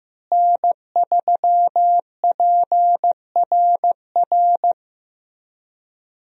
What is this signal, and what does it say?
Morse code, heard as a keyed tone, about 15 wpm, 700 Hz. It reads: N3PRR
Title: USA Call Signs - 5 Characters - Repeated 3x - 15wpm